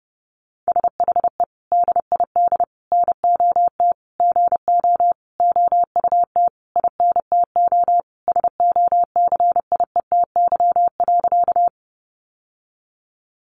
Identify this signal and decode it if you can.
Morse code, heard as a keyed tone, about 30 wpm, 700 Hz. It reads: SHE DID NOT GO OUT INTO SOCIETY.
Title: Sentences from Top 700 Words 30wpm